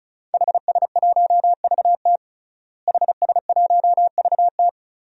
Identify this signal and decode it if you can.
Morse code, heard as a keyed tone, about 35 wpm, 700 Hz. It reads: HS1VT HS1VT
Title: DXCC-100 Call Signs - 5 Characters - Repeated 3x - 35wpm